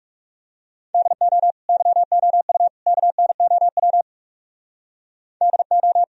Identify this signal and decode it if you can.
Morse code, heard as a keyed tone, about 37 wpm, 700 Hz. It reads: DO YOU KNOW DO
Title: Speed-Racing - Sets of 3 Words 30wpm